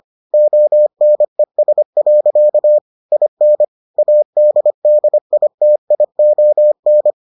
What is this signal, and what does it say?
Morse code, heard as a keyed tone, about 25 wpm, 600 Hz. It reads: ONES. IN ADDITION